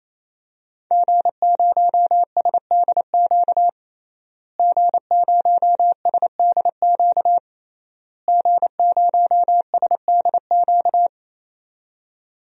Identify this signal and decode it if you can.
Morse code, heard as a keyed tone, about 28 wpm, 700 Hz. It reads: G0SDQ G0SDQ G0SDQ
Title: DXCC-100 Call Signs - 5 Characters - Repeated 3x - 28wpm